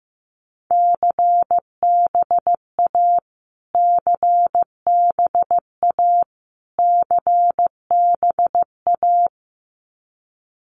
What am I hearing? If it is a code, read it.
Morse code, heard as a keyed tone, about 15 wpm, 700 Hz. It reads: CBA CBA CBA
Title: Abbreviations Repeated 3x - 15wpm